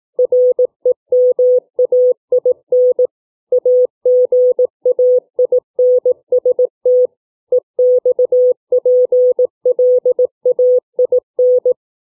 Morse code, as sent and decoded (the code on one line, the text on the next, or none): .-. -- .- .. -. / .- --. .- .. -. ... - / . -..- .--. .-.. .- .. -.
REMAIN AGAINST EXPLAIN